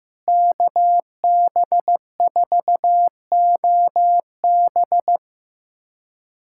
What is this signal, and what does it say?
Morse code, heard as a keyed tone, about 15 wpm, 700 Hz. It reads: KB4OB
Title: USA Callsign DE Callsign - 5 Characters - 15wpm